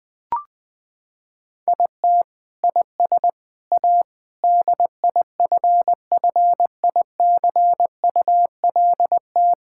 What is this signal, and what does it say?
Morse code, heard as a keyed tone, about 21 wpm, 700 Hz. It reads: IT IS A DIFFICULT
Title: Sentences from Top 1000 Words - Review Part 6 - 20wpm